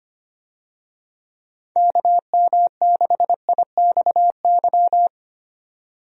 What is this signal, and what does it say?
Morse code, heard as a keyed tone, about 25 wpm, 700 Hz. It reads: KM6IXY
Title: USA Call Signs - 6 Characters - 25wpm